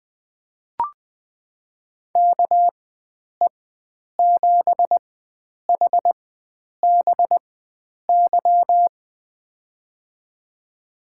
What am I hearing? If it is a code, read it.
Morse code, heard as a keyed tone, about 20 wpm, spaced as for 9 wpm, 700 Hz. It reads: KE7HBY
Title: USA Call Signs - 6 Characters - 4x Character Spacing - 20wpm